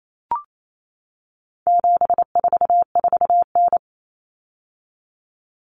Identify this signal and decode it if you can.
Morse code, heard as a keyed tone, about 28 wpm, 700 Hz. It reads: 744N
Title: Common Cut Numbers - 4-Digit 28wpm